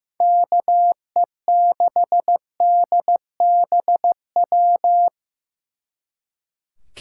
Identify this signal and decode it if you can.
Morse code, heard as a keyed tone, about 18 wpm, 700 Hz. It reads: KE6DBW
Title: Speed Racing - USA 2x3 Call Signs 15wpm